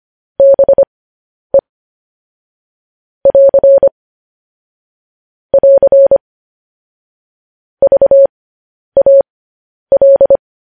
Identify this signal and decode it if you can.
Morse code, heard as a keyed tone, about 25 wpm, spaced as for 10 wpm, 550 Hz. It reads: BE <AR> <AR> VAL
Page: news headlines in morse code at 10 wpm